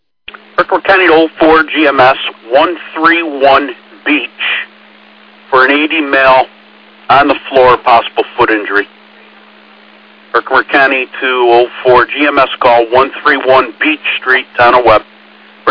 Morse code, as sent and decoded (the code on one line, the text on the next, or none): none